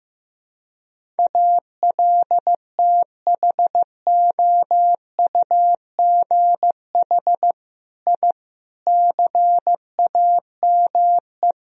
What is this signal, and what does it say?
Morse code, heard as a keyed tone, about 15 wpm, 700 Hz. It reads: ALTHOUGH I CAME
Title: Sentences from Top 1000 Words - Review Part 3 - 15wpm